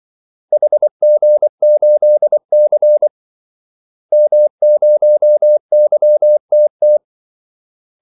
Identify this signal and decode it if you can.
Morse code, heard as a keyed tone, about 24 wpm, 600 Hz. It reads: HG8C M0YTT
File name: Callsigns-24wpm